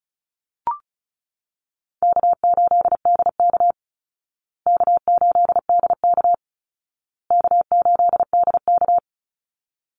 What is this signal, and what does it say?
Morse code, heard as a keyed tone, about 34 wpm, 700 Hz. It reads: K8DK K8DK K8DK
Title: Easy Call Signs Repeated 3x - 35wpm